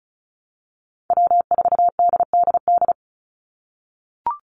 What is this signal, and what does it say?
Morse code, heard as a keyed tone, about 34 wpm, 700 Hz. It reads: W4DDD
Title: Call Signs - 35wpm